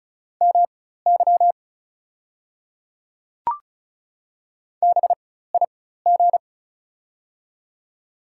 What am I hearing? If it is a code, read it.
Morse code, heard as a keyed tone, about 36 wpm, spaced as for 16 wpm, 700 Hz. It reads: MY BIG
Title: 4x Character Spacing - Top 300 Words - 35wpm